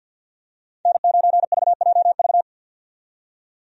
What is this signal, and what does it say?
Morse code, heard as a keyed tone, about 51 wpm, 700 Hz. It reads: N9VJV
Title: USA Call Signs - 5 Characters - 50wpm